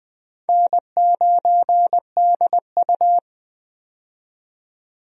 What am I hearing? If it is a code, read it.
Morse code, heard as a keyed tone, about 20 wpm, 700 Hz. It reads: N9DU
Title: USA Call Signs - 4 Characters - 20wpm